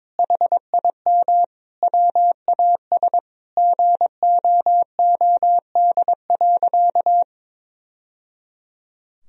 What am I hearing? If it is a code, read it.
Morse code, heard as a keyed tone, about 22 wpm, 700 Hz. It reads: HIM WAS GOOD.
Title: Sentences from Top 1000 Words - Review Part 4 - 22wpm